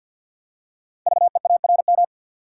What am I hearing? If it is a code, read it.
Morse code, heard as a keyed tone, about 50 wpm, 700 Hz. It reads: VEARM
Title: Code Groups - 5 Letters - 50wpm